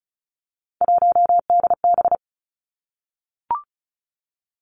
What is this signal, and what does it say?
Morse code, heard as a keyed tone, about 36 wpm, 700 Hz. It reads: 1DB
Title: Three Challenging Letter-Number Combinations 35wpm